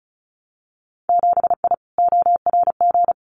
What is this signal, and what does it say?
Morse code, heard as a keyed tone, about 34 wpm, 700 Hz. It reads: 7I ORG
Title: Winter Field Day Exchanges Encore 35wpm